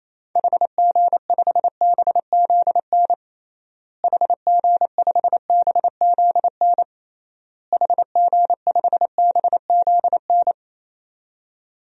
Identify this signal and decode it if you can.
Morse code, heard as a keyed tone, about 28 wpm, 700 Hz. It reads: HG5BZN HG5BZN HG5BZN
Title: DXCC-100 Call Signs - 6 Characters - Repeated 3x - 28wpm